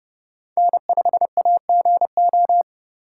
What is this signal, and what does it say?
Morse code, heard as a keyed tone, about 28 wpm, 700 Hz. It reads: N5AGO